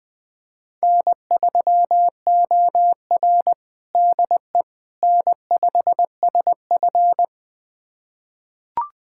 Morse code, heard as a keyed tone, about 20 wpm, 700 Hz. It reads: N3OR DE N5SF
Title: USA Callsign DE Callsign - 4 Characters - 20wpm